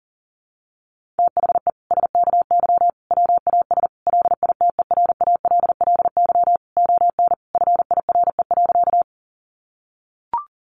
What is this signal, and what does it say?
Morse code, heard as a keyed tone, about 40 wpm, 700 Hz. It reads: THE SKY WAS LITERALLY ON FIRE.
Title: Sentences from Top 1400 Words 40wpm